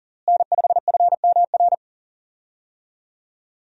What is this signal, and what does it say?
Morse code, heard as a keyed tone, about 39 wpm, 700 Hz. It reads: N5FMR
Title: USA Call Signs - 5 Characters - Repeated 3x - 40wpm